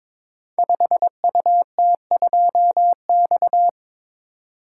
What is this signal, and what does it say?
Morse code, heard as a keyed tone, about 22 wpm, 700 Hz. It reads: 5UT2X
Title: Code Groups - 5 Letter:Numbers - 22wpm